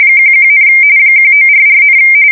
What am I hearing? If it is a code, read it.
RTTY, about 45.45 baud. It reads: 11, 1944,
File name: fsk170 / battleshipnj